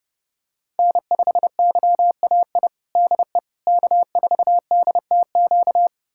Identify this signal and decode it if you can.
Morse code, heard as a keyed tone, about 32 wpm, 700 Hz. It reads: N5YAI DE K4DTQ
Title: USA Callsign DE Callsign 30wpm